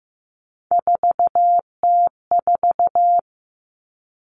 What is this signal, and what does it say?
Morse code, heard as a keyed tone, about 15 wpm, 700 Hz. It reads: 4T4
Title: Common Cut Numbers - 3-Digit 15wpm